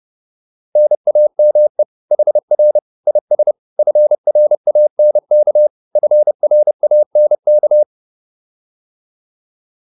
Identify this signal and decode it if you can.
Morse code, heard as a keyed tone, about 30 wpm, 600 Hz. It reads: NAME HR IS FRANK FRANK
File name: QSO208 30